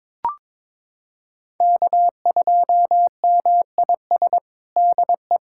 Morse code, heard as a keyed tone, about 22 wpm, 700 Hz. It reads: K2MIS DE
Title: USA Callsign DE Callsign - 5 Characters - 22wpm